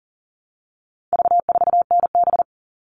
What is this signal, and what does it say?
Morse code, heard as a keyed tone, about 40 wpm, 700 Hz. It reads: V4NB
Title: All Cut Numbers - 4-Digit 40wpm